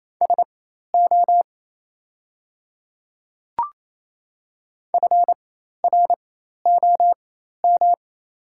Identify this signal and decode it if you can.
Morse code, heard as a keyed tone, about 27 wpm, spaced as for 13 wpm, 700 Hz. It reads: SO FROM
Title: 4x Character Spacing - Top 100 Words - 28wpm